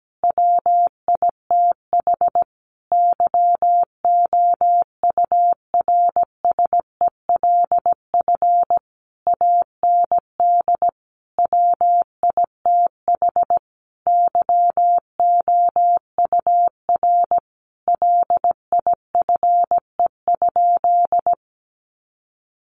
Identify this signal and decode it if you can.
Morse code, heard as a keyed tone, about 17 wpm, 700 Hz. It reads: WITH YOURSELF AND WITH YOUR LIFE?
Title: Sentences from Top 900 Words 17wpm